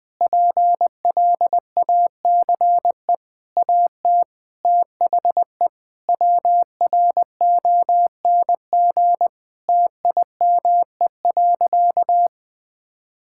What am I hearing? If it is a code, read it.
Morse code, heard as a keyed tone, about 20 wpm, 700 Hz. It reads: PLACE AT THE WRONG TIME.
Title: Sentences with 7-Word Phrases - Spoken Beginning - 20wpm